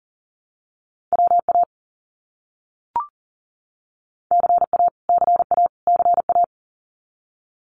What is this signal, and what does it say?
Morse code, heard as a keyed tone, about 38 wpm, 700 Hz. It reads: WA CA CA CA
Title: US States Repeated 3x 40wpm